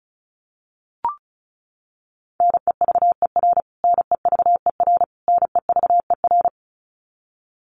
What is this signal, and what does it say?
Morse code, heard as a keyed tone, about 34 wpm, 700 Hz. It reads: NEVER NEVER NEVER
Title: Top 500 Words Repeated 35wpm